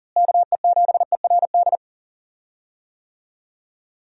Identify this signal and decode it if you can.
Morse code, heard as a keyed tone, about 40 wpm, 700 Hz. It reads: KE7ERD
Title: USA Call Signs - 6 Characters - Repeated 2x - 40wpm